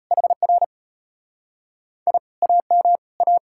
Morse code, heard as a keyed tone, about 38 wpm, 700 Hz. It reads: HR I AM A